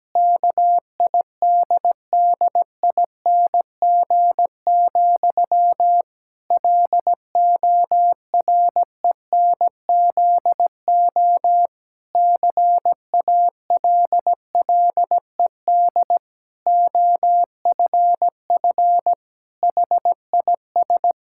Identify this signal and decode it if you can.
Morse code, heard as a keyed tone, about 17 wpm, 700 Hz. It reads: KIDDING, LORENZO CALLED OFF HIS
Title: Pangram Challenge v2 - 17wpm